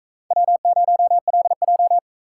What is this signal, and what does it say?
Morse code, heard as a keyed tone, about 43 wpm, 700 Hz. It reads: W0LJ